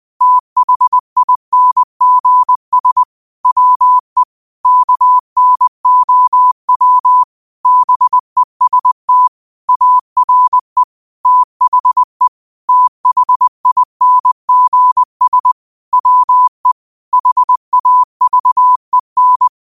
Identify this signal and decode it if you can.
Morse code, heard as a keyed tone, about 20 wpm, 1000 Hz. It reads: THINGS WE KNOW BEST ARE THE THINGS WE HAVEN
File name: qotdfiles / 20wpm